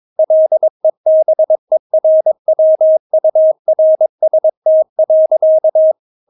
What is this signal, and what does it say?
Morse code, heard as a keyed tone, about 22 wpm, 630 Hz. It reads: LEBERWURST.
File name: Witz3 22wpm